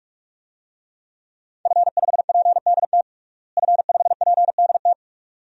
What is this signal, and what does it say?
Morse code, heard as a keyed tone, about 41 wpm, 700 Hz. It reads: F5PDT F5PDT
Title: DXCC-100 Call Signs - 5 Characters - Repeated 2x - 45wpm